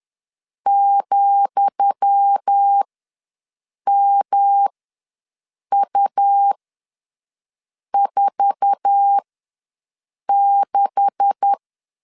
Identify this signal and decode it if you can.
Morse code, heard as a keyed tone, about 10 wpm, spaced as for 6 wpm, 800 Hz. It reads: ,MU46
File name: LEZIONI / 10 SIGNS / 10D COMA ws 10wpm 4314